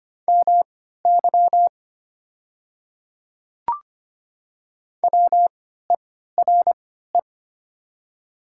Morse code, heard as a keyed tone, about 25 wpm, spaced as for 14 wpm, 690 Hz. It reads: MY WERE
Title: 3x Character Spacing - Top 100 Words - 25wpm